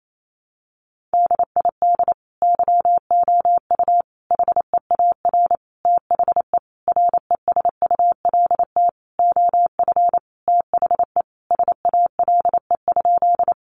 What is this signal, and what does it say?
Morse code, heard as a keyed tone, about 28 wpm, 700 Hz. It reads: DID YOU HEAR THE RESULT OF THE SALE?
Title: Sentences from Top 1400 Words 28wpm